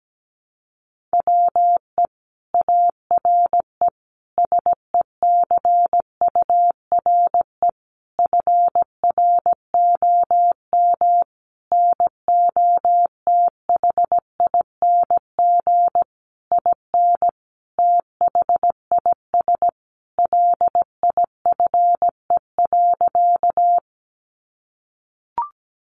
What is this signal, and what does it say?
Morse code, heard as a keyed tone, about 17 wpm, 700 Hz. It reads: WE ARE SECURE FROM NOTHING IN THIS LIFE.
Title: Sentences from Top 1200 Words 17wpm